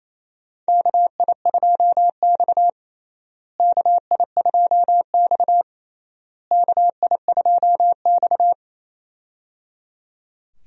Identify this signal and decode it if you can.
Morse code, heard as a keyed tone, about 28 wpm, 700 Hz. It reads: KI2X KI2X KI2X
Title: USA Call Signs - 4 Characters - Repeated 3x - 28wpm